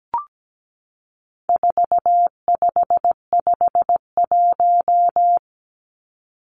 Morse code, heard as a keyed tone, about 17 wpm, 700 Hz. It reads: 4551
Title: Numbers - 4-Digit 17wpm